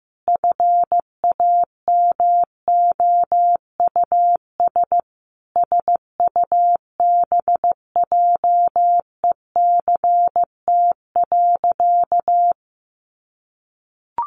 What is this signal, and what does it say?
Morse code, heard as a keyed tone, about 15 wpm, 700 Hz. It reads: FAMOUS SUBJECT.
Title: Sentences from Top 1500 Words 15wpm